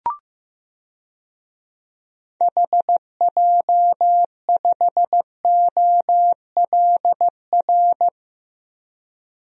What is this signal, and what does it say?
Morse code, heard as a keyed tone, about 15 wpm, 700 Hz. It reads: HJ5OLR
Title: DXCC-100 Call Signs - 6 Characters - Repeated 3x - 15wpm